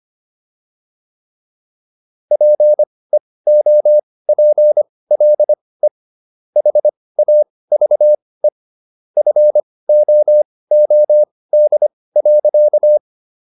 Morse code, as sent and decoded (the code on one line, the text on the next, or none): .--. --- .--. .-.. / .... .- ...- . / ..-. --- --- -.. .-.-.-
PEOPLE HAVE FOOD.